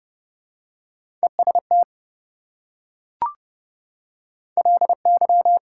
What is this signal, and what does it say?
Morse code, heard as a keyed tone, about 29 wpm, 700 Hz. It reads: EST LY
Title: Prefixes and Suffixes 30wpm